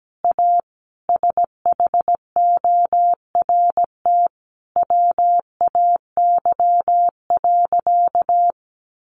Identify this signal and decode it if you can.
Morse code, heard as a keyed tone, about 17 wpm, 700 Hz. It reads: A SHORT WAY.
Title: Sentences from Top 1400 Words 17wpm